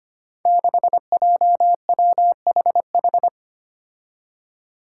Morse code, heard as a keyed tone, about 25 wpm, 700 Hz. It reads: 6JWHH
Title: Five Challenging Letter-Number Combinations 25wpm